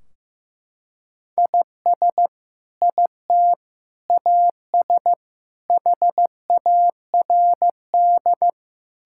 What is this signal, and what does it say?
Morse code, heard as a keyed tone, about 15 wpm, 700 Hz. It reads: IS IT AS HARD
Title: Sentences from Top 1600 Words 15wpm